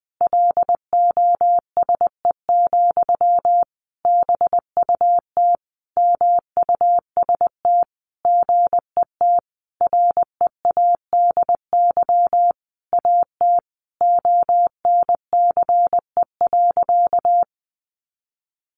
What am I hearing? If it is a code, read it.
Morse code, heard as a keyed tone, about 20 wpm, 700 Hz. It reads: LOSE, BUT MUST GET READY AT ONCE.
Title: Sentences from Top 1100 Words 20wpm